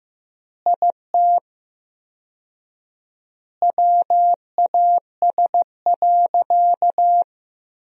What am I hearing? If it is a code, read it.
Morse code, heard as a keyed tone, about 15 wpm, 700 Hz. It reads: IT WAS.
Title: 4x Word-Spacing - Sentences from Top 100 Words - 15wpm